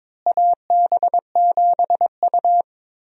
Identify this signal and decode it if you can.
Morse code, heard as a keyed tone, about 22 wpm, 700 Hz. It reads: AB7U
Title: USA Call Signs - 4 Characters - Repeated 3x - 22wpm